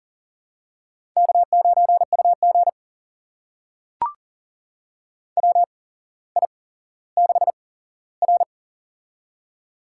Morse code, heard as a keyed tone, about 40 wpm, spaced as for 17 wpm, 700 Hz. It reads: K9UG WI6R
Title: USA Call Signs - 4 Characters - 8x Character Spacing - 40wpm